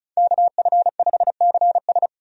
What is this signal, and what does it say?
Morse code, heard as a keyed tone, about 36 wpm, 700 Hz. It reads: KF5CS
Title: USA Call Signs - 5 Characters - 35wpm